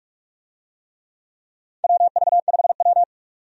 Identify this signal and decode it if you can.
Morse code, heard as a keyed tone, about 46 wpm, 700 Hz. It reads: WV5W
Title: USA Call Signs - 4 Characters - 45wpm